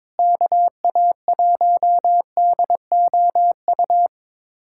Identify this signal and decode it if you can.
Morse code, heard as a keyed tone, about 22 wpm, 700 Hz. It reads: KA1DOU
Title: USA Callsign DE Callsign 22wpm